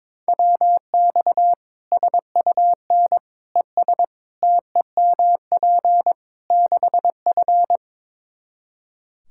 Morse code, heard as a keyed tone, about 22 wpm, 700 Hz. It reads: WX SUN ES TEMP 6F
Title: Practice QSO - Exchange 2 - Info - 22wpm